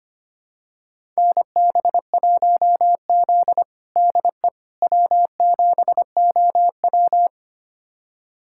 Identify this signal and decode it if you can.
Morse code, heard as a keyed tone, about 25 wpm, 700 Hz. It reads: NB1Z DE W7OW
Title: USA Callsign DE Callsign - 4 Characters - 25wpm